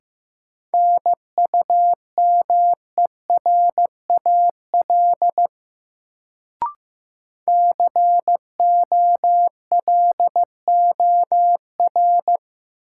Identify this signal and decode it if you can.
Morse code, heard as a keyed tone, about 15 wpm, 700 Hz. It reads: NUMERAL COLOR
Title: Sentences from Top 1000 Words - Review Part 2 - New Word Drill - 15wpm